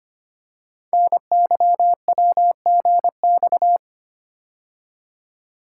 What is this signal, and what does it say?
Morse code, heard as a keyed tone, about 25 wpm, 700 Hz. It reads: NYWGX
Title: Code Groups - 5 Letter:Numbers - 25wpm